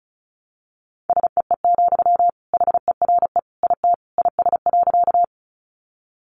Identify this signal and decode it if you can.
Morse code, heard as a keyed tone, about 34 wpm, 700 Hz. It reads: SEE, HERE IT IS.
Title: Sentences from Top 200 Words 35wpm